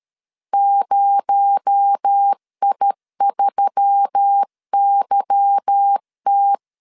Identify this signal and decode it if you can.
Morse code, heard as a keyed tone, about 13 wpm, 800 Hz. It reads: 0I3YT
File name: CW / DET / 1 9 1500C S B 12wpm 3819